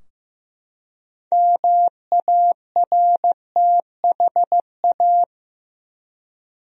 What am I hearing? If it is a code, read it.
Morse code, heard as a keyed tone, about 16 wpm, 700 Hz. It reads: MARTHA
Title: USA Common Names - 15wpm